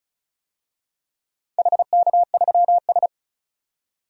Morse code, heard as a keyed tone, about 34 wpm, 700 Hz. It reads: HK3S
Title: DXCC-100 Call Signs - 4 Characters - 35wpm